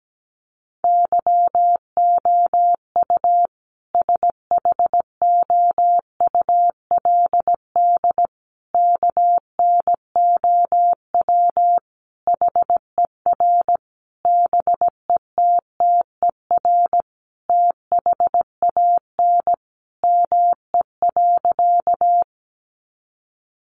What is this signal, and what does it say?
Morse code, heard as a keyed tone, about 17 wpm, 700 Hz. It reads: YOU SHOULD KNOW HER BETTER THAN ME.
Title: Sentences from Top 400 Words 17wpm